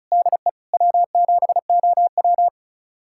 Morse code, heard as a keyed tone, about 36 wpm, 700 Hz. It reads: DE W7OW
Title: USA Callsign DE Callsign - 4 Characters - 35wpm